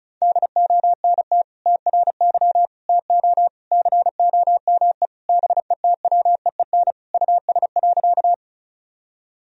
Morse code, heard as a keyed tone, about 36 wpm, 700 Hz. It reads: DONT TRY TO COME BETWEEN US.
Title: Sentences from Top 1000 Words - Review Part 1 - 35wpm